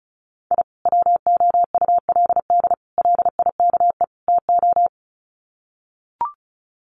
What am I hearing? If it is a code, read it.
Morse code, heard as a keyed tone, about 34 wpm, 700 Hz. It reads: I WOULD LIKE TO
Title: Sets of 4 Words 35wpm